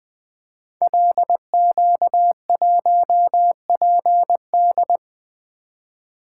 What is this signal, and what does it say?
Morse code, heard as a keyed tone, about 20 wpm, 700 Hz. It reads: LQ1PD
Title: Code Groups - 5 Letter:Numbers - 20wpm